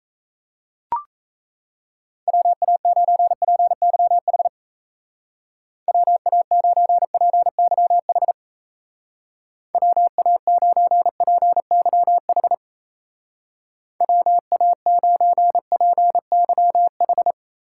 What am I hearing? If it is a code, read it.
Morse code, heard as a keyed tone, about 37 wpm, 700 Hz. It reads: WA9PYH WA9PYH WA9PYH WA9PYH